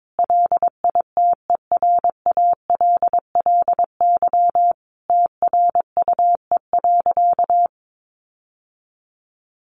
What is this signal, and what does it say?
Morse code, heard as a keyed tone, about 22 wpm, 700 Hz. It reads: LITERALLY TRUE.
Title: Sentences from Top 1400 Words 22wpm